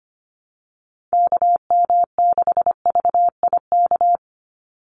Morse code, heard as a keyed tone, about 25 wpm, 700 Hz. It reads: KM6VIK
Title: USA Call Signs 25wpm